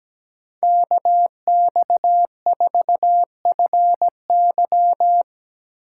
Morse code, heard as a keyed tone, about 17 wpm, 700 Hz. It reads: KX4FY
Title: USA Call Signs - 5 Characters - 2x Character Spacing - 17wpm